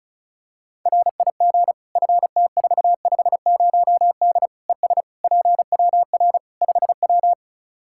Morse code, heard as a keyed tone, about 36 wpm, 700 Hz. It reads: RIG FT450D ES PWR 5W
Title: Practice QSO - Exchange 2 - Info - 35wpm